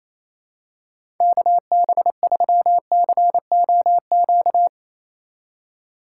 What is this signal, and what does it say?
Morse code, heard as a keyed tone, about 27 wpm, 700 Hz. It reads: KB3COQ
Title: USA Call Signs - 6 Characters - 4x Character Spacing - 28wpm